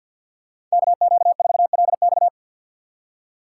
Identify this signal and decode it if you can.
Morse code, heard as a keyed tone, about 52 wpm, 700 Hz. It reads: KQ4LX
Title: USA Call Signs - 5 Characters - 2x Character Spacing - 50wpm